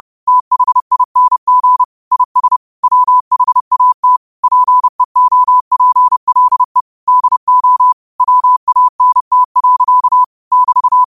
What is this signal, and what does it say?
Morse code, heard as a keyed tone, about 30 wpm, 1000 Hz. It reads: THING IS WHAT PEOPLE DO WANT. <BT>